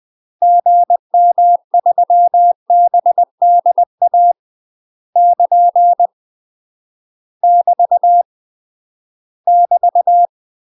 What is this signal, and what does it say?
Morse code, heard as a keyed tone, about 20 wpm, 700 Hz. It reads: GM3BDA <KN> <BT> <BT>
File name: UK MASTER QSOs TRAINING/20WPM